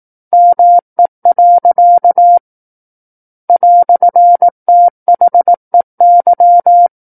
Morse code, heard as a keyed tone, about 18 wpm, 700 Hz. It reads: ME. "THEY
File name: 18 WPM